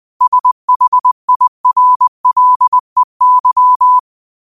Morse code, heard as a keyed tone, about 20 wpm, 1000 Hz. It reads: SHIRLEY
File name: qotdfiles / 20wpm